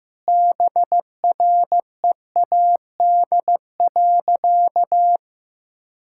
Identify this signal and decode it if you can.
Morse code, heard as a keyed tone, about 15 wpm, 700 Hz. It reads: BREAD.
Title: Sentences from Top 1600 Words 15wpm